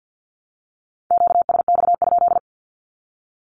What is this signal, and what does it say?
Morse code, heard as a keyed tone, about 50 wpm, 700 Hz. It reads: QSX?
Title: Q-Codes Repeated 3x 50wpm